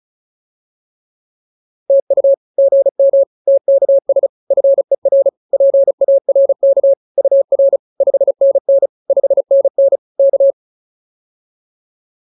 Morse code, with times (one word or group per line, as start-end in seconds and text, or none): 1.9-2.3 TU
2.6-3.2 GM
3.5-4.3 TKS
4.5-5.3 FER
5.5-6.9 PARK
7.2-7.8 UR
8.0-8.9 5NN
9.1-10.0 5NN
10.2-10.5 K